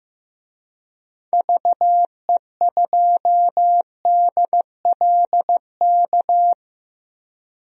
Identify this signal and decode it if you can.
Morse code, heard as a keyed tone, about 15 wpm, 700 Hz. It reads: VE2DLK
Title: DXCC-100 Call Signs - 6 Characters - 15wpm